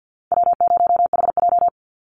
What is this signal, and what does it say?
Morse code, heard as a keyed tone, about 50 wpm, 700 Hz. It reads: W0HJ
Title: USA Call Signs 50wpm